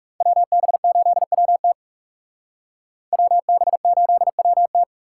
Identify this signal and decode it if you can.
Morse code, heard as a keyed tone, about 36 wpm, 700 Hz. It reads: WB8WT WB8WT
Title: Speed Racing - USA 2x2 Call Signs 30wpm